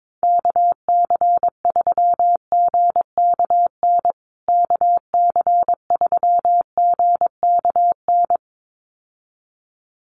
Morse code, heard as a keyed tone, about 22 wpm, 700 Hz. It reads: KC3GKN KC3GKN
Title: USA Call Signs Repeated 2x 22wpm